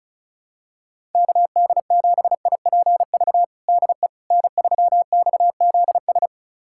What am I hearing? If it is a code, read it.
Morse code, heard as a keyed tone, about 34 wpm, 700 Hz. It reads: KD7IPV DE N3XZS
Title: USA Callsign DE Callsign 35wpm